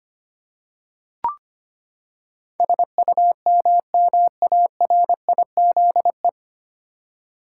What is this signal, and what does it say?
Morse code, heard as a keyed tone, about 25 wpm, 700 Hz. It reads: SUMMARIZE
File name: Nine Letter Words - 25wpm